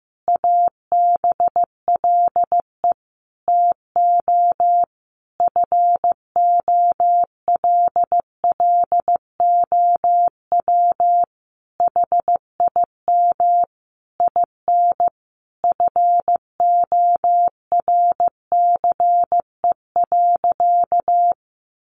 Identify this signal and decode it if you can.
Morse code, heard as a keyed tone, about 15 wpm, 700 Hz. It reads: ABLE TO FOLLOW HIM IN FORCE.
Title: Sentences from Top 700 Words 15wpm